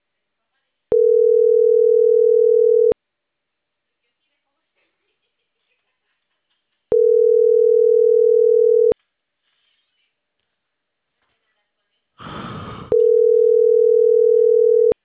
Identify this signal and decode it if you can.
Ringback tone (North America), 440 + 480 Hz.